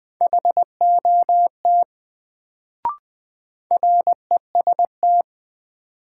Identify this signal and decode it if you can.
Morse code, heard as a keyed tone, about 20 wpm, 700 Hz. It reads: HOT REST
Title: Sentences from Top 500 Words - New Word Drill - 20wpm